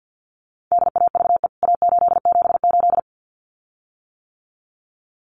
Morse code, heard as a keyed tone, about 48 wpm, 700 Hz. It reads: DAVE A878
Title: CWT Exchange 50wpm